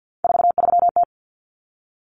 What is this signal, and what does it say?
Morse code, heard as a keyed tone, about 50 wpm, 700 Hz. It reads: 43T
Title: Common Cut Numbers - 3-Digit 50wpm